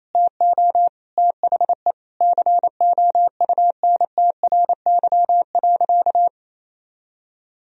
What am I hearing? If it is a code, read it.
Morse code, heard as a keyed tone, about 28 wpm, 700 Hz. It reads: TO THE COUNTRY.
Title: Sentences from Top 1000 Words - Review Part 1 - 28wpm